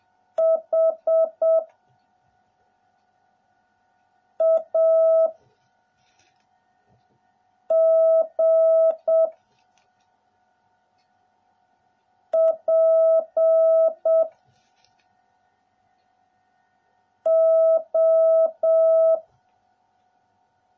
Morse code, heard as a keyed tone, about 7 wpm, 650 Hz. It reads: H A G P O